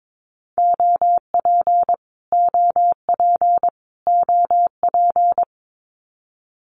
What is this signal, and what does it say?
Morse code, heard as a keyed tone, about 22 wpm, 700 Hz. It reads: OP OP OP
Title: Abbreviations Repeated 3x - 22wpm